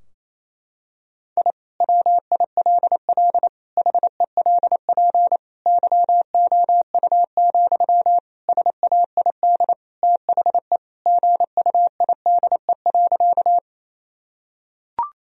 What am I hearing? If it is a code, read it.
Morse code, heard as a keyed tone, about 27 wpm, 700 Hz. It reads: I WILL HELP YOU, SAID THE GUIDE.
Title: Sentences from Top 1000 Words - Review Part 3 - 28wpm